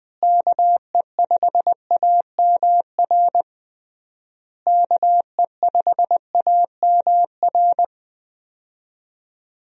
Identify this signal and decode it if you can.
Morse code, heard as a keyed tone, about 20 wpm, 700 Hz. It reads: KE5AMR KE5AMR
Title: USA Call Signs - 6 Characters - Repeated 3x - 20wpm